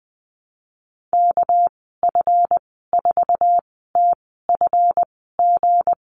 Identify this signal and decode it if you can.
Morse code, heard as a keyed tone, about 20 wpm, spaced as for 14 wpm, 700 Hz. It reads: KF4TFG
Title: 245 Lesson 39g - Letter z - Call Signs 20wpm